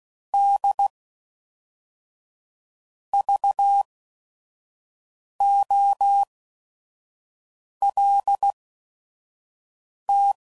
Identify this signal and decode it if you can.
Morse code, heard as a keyed tone, about 16 wpm, spaced as for 5 wpm, 800 Hz. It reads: D VOLT
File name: CW Code Groups @5WPM / Technical Subjects / XC